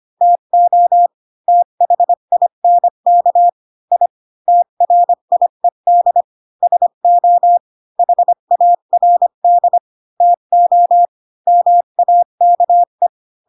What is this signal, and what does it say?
Morse code, heard as a keyed tone, about 25 wpm, 700 Hz. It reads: TO THINK I TRIED SO HARD TO MAKE
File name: Pinocchio / Pinocchio ch3 20 25 ik1hge